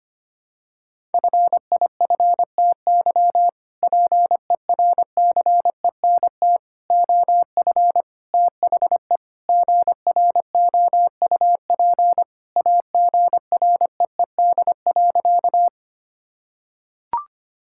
Morse code, heard as a keyed tone, about 25 wpm, 700 Hz. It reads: FIFTY PERCENT OF THE GROUP AGREED.
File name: practice / 25wpm / Sentences with 3-Word Phrases - 25wpm